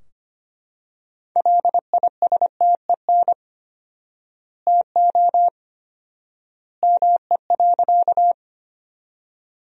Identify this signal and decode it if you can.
Morse code, heard as a keyed tone, about 25 wpm, 700 Hz. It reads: LISTEN TO ME.
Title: 4x Word-Spacing - Sentences from Top 400 Words - 25wpm